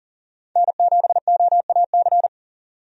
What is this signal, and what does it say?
Morse code, heard as a keyed tone, about 40 wpm, 700 Hz. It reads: N7OAC
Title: USA Call Signs - 5 Characters - 40wpm